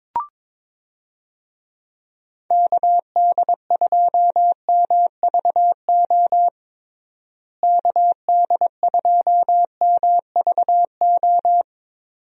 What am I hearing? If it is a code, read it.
Morse code, heard as a keyed tone, about 22 wpm, 700 Hz. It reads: KD2MVO KD2MVO
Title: USA Call Signs - 6 Characters - Repeated 2x - 22wpm